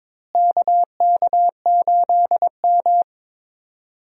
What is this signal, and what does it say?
Morse code, heard as a keyed tone, about 22 wpm, 700 Hz. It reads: KK8M
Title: USA Call Signs - 4 Characters - Repeated 3x - 22wpm